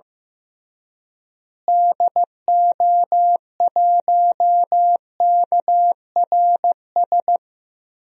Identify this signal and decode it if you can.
Morse code, heard as a keyed tone, about 16 wpm, 700 Hz. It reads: DO1KRS
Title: DXCC-100 Call Signs - 6 Characters - Repeated 3x - 15wpm